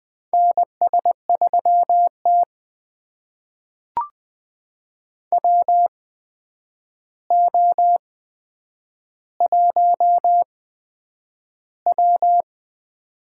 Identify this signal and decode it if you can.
Morse code, heard as a keyed tone, about 20 wpm, spaced as for 9 wpm, 700 Hz. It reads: NS3T WO1W
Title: USA Call Signs - 4 Characters - 8x Character Spacing - 20wpm